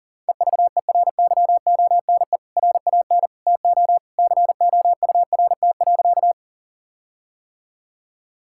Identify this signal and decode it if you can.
Morse code, heard as a keyed tone, about 40 wpm, 700 Hz. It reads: EVERYONE RAN TO COURT.
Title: Sentences from Top 1600 Words 40wpm